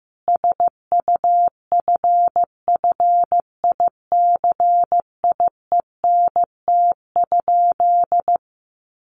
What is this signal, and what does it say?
Morse code, heard as a keyed tone, about 15 wpm, 700 Hz. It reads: SUFFICIENT?
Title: Sentences from Top 1400 Words 15wpm